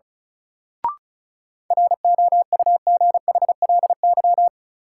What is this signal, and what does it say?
Morse code, heard as a keyed tone, about 34 wpm, 700 Hz. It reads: ROUGHLY
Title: Sentences from Top 1500 Words - New Word Drill Alt - 35wpm